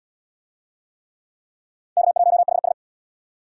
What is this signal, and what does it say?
Morse code, heard as a keyed tone, about 74 wpm, 700 Hz. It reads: K1HS